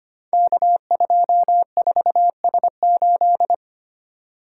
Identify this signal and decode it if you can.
Morse code, heard as a keyed tone, about 25 wpm, 700 Hz. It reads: K24S8
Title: Code Groups - 5 Letter:Numbers - 25wpm